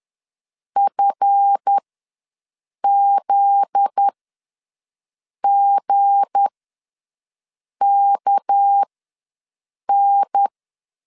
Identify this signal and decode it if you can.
Morse code, heard as a keyed tone, about 10 wpm, spaced as for 5 wpm, 800 Hz. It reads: FZGKN
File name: LEZIONI / 7 GF9Z / 7 GF9Z ws 10wpm 3919